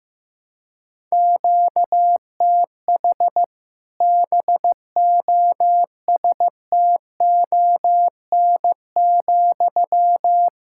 Morse code, heard as a keyed tone, about 15 wpm, 700 Hz. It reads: QTH BOSTON,
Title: Practice QSO - Exchange 1 - Beginning - 15wpm